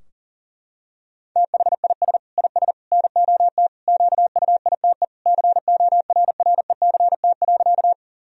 Morse code, heard as a keyed tone, about 37 wpm, 700 Hz. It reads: THIS IS NOT QUITE CORRECT.
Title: Sentences from Top 1000 Words - Review Part 2 - 40wpm